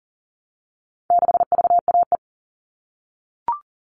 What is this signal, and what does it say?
Morse code, heard as a keyed tone, about 40 wpm, 700 Hz. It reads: 6VAE
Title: All Cut Numbers - 4-Digit 40wpm